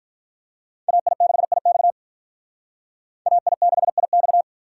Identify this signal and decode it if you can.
Morse code, heard as a keyed tone, about 46 wpm, 700 Hz. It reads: AI6IX AI6IX